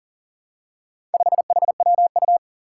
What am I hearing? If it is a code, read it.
Morse code, heard as a keyed tone, about 40 wpm, 700 Hz. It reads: 5HWU
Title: Four Challenging Letter-Number Combinations 40wpm